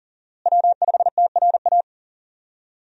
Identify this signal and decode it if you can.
Morse code, heard as a keyed tone, about 40 wpm, 700 Hz. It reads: W5TRA
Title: USA Call Signs - 5 Characters - Repeated 2x - 40wpm